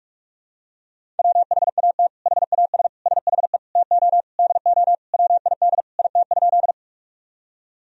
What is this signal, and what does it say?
Morse code, heard as a keyed tone, about 46 wpm, 700 Hz. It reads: WHAT HAS SHE TO DO WID IT?
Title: Sentences from Top 100 Words and Abbreviations 45wpm